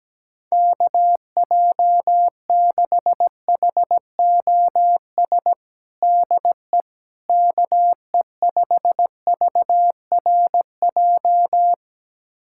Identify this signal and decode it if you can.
Morse code, heard as a keyed tone, about 17 wpm, 700 Hz. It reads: KJ6HOS DE KE5VRJ
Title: USA Callsign DE Callsign - 6 Characters - 17wpm